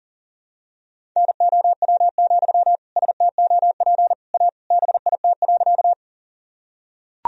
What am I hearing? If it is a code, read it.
Morse code, heard as a keyed tone, about 40 wpm, 700 Hz. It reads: NOW, STOP A BIT.
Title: Sentences from Top 1000 Words - Review Part 1 - 40wpm